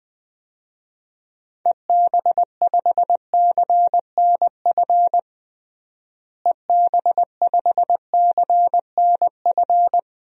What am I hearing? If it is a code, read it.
Morse code, heard as a keyed tone, about 20 wpm, 700 Hz. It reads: EB5CNF EB5CNF
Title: DXCC-100 Call Signs - 6 Characters - Repeated 2x - 20wpm